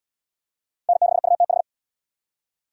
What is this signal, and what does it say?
Morse code, heard as a keyed tone, about 72 wpm, 700 Hz. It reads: N6UTH